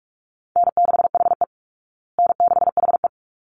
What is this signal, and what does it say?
Morse code, heard as a keyed tone, about 44 wpm, 700 Hz. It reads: N6HE N6HE
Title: Call Signs Repeated 3x - 45wpm